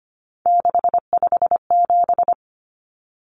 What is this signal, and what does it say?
Morse code, heard as a keyed tone, about 25 wpm, 690 Hz. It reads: 657
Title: Numbers - 3-Digit 25wpm